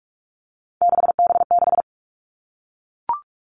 Morse code, heard as a keyed tone, about 44 wpm, 700 Hz. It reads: 6B6
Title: All Cut Numbers - 3-Digit 45wpm